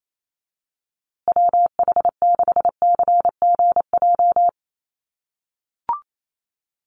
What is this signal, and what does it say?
Morse code, heard as a keyed tone, about 28 wpm, 700 Hz. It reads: WH6CGJ
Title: USA Call Signs Repeated 3x 28wpm